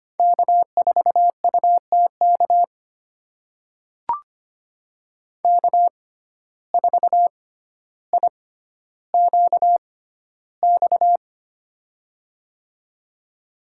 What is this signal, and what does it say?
Morse code, heard as a keyed tone, about 25 wpm, spaced as for 13 wpm, 700 Hz. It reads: K4UTK K4IQX
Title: USA Call Signs - 5 Characters - 6x Character Spacing - 25wpm